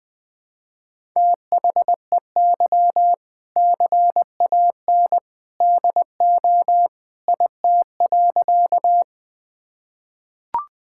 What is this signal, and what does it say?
Morse code, heard as a keyed tone, about 21 wpm, 700 Hz. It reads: THEY CAN DO IT.
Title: Sentences with 2-Word Phrases - 20wpm